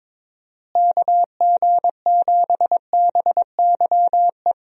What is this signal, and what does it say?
Morse code, heard as a keyed tone, about 22 wpm, 700 Hz. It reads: KG7BYE
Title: USA Call Signs - 6 Characters - Repeated 2x - 22wpm